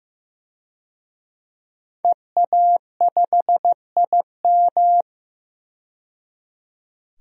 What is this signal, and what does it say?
Morse code, heard as a keyed tone, about 15 wpm, 700 Hz. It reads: EA5IM
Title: DXCC-100 Call Signs - 5 Characters - 15wpm